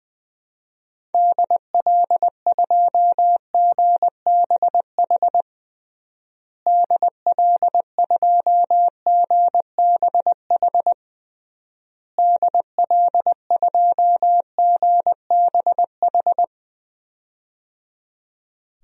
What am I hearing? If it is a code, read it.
Morse code, heard as a keyed tone, about 21 wpm, 700 Hz. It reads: DL2GBH DL2GBH DL2GBH
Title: DXCC-100 Call Signs - 6 Characters - Repeated 3x - 20wpm